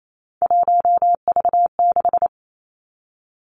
Morse code, heard as a keyed tone, about 28 wpm, 700 Hz. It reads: 1V6
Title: Three Challenging Letter-Number Combinations 28wpm